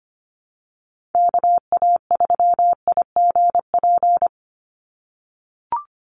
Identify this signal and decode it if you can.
Morse code, heard as a keyed tone, about 25 wpm, 700 Hz. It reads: KA3IGP
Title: USA Call Signs 25wpm